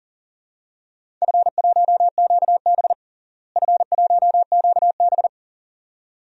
Morse code, heard as a keyed tone, about 37 wpm, 700 Hz. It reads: F1QB F1QB
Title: DXCC-100 Call Signs - 4 Characters - Repeated 2x - 40wpm